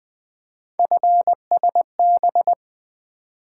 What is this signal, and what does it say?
Morse code, heard as a keyed tone, about 24 wpm, 700 Hz. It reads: FSB
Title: Speed-Racing - Three Letter-Number Combinations - 20wpm